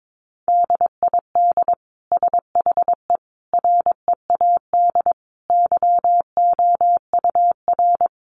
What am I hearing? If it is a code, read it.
Morse code, heard as a keyed tone, about 22 wpm, 700 Hz. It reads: DID SHE READ YOUR
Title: Sentences from Top 1400 Words 22wpm